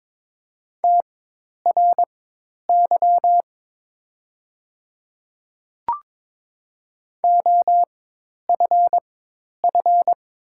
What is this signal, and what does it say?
Morse code, heard as a keyed tone, about 22 wpm, spaced as for 10 wpm, 700 Hz. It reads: TRY OFF
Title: 4x Character Spacing - Top 500 Words - 22wpm